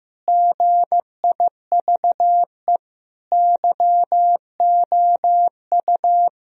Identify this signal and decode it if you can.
Morse code, heard as a keyed tone, about 15 wpm, 700 Hz. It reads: GIVE YOU
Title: Sentences from Top 1000 Words - Review Part 1 - 15wpm